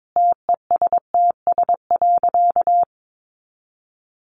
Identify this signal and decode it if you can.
Morse code, heard as a keyed tone, about 22 wpm, 700 Hz. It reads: TESTS.
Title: Additional Easy Phrases 22wpm